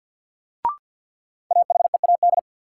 Morse code, heard as a keyed tone, about 53 wpm, 700 Hz. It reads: AHEAD
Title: Sentences from Top 1100 Words - New Word Drill Alt - 50wpm